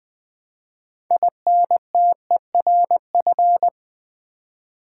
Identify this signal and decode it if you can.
Morse code, heard as a keyed tone, about 20 wpm, 700 Hz. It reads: INTERF
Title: Build A Word - 6 Letters At A Time - 12-Letter Words - 20wpm